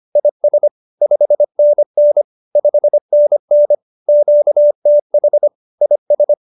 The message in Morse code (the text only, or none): IS 5NN 5NN QTH IS